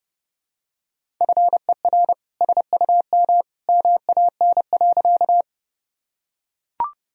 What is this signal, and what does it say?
Morse code, heard as a keyed tone, about 31 wpm, 700 Hz. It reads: FER SUM MAN.
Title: Sentences from Top 200 Words and Abbreviations 30wpm